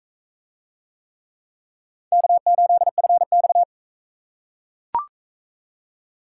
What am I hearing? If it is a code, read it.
Morse code, heard as a keyed tone, about 41 wpm, 700 Hz. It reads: K8FX